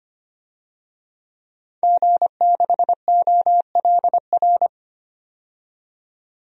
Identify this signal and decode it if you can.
Morse code, heard as a keyed tone, about 25 wpm, 700 Hz. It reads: G6OLR
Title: DXCC-100 Call Signs - 5 Characters - 25wpm